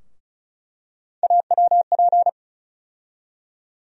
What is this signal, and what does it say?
Morse code, heard as a keyed tone, about 36 wpm, 700 Hz. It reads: AWP
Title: USA Call Sign Postfixes - 3 Letters 35wpm